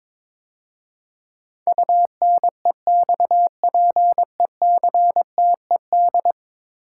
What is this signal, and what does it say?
Morse code, heard as a keyed tone, about 22 wpm, 700 Hz. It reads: UNEXPECTED
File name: practice / 22wpm / 10 Letter Words - First 3 Letters Given - 22wpm